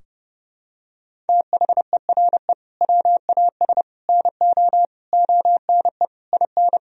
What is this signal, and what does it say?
Morse code, heard as a keyed tone, about 30 wpm, 700 Hz. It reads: THERE WAS NO ONE IN
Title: Sentences from Top 1000 Words - Review Part 4 - 30wpm